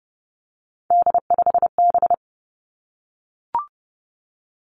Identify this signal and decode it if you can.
Morse code, heard as a keyed tone, about 31 wpm, 700 Hz. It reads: D5B
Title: Three Challenging Letter-Number Combinations 30wpm